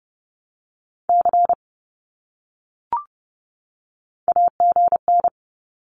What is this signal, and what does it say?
Morse code, heard as a keyed tone, about 30 wpm, 700 Hz. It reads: C AGN
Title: Abbreviations 30wpm